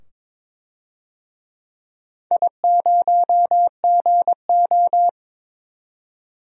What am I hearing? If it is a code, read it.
Morse code, heard as a keyed tone, about 22 wpm, 700 Hz. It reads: I0GO
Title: DXCC-100 Call Signs - 4 Characters - 22wpm